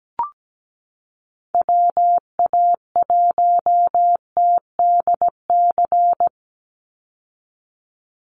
Morse code, heard as a keyed tone, about 17 wpm, 700 Hz. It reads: WA1TDC
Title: USA Call Signs 17wpm